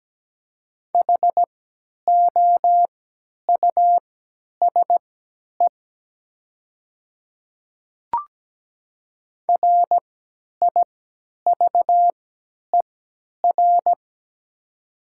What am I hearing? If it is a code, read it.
Morse code, heard as a keyed tone, about 17 wpm, spaced as for 10 wpm, 690 Hz. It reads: HOUSE RIVER
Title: 3x Character Spacing - Top 300 Words - 17wpm